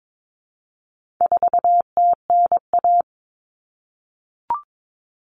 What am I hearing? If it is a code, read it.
Morse code, heard as a keyed tone, about 22 wpm, 700 Hz. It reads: 4TNA
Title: Common Cut Numbers - 4-Digit 22wpm